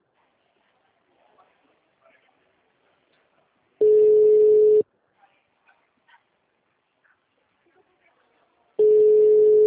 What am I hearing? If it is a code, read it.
Ringback tone (Europe, 425 Hz).